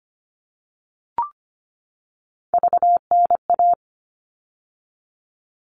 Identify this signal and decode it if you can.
Morse code, heard as a keyed tone, about 25 wpm, 700 Hz. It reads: VNA
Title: All Cut Numbers - 3-Digit 25wpm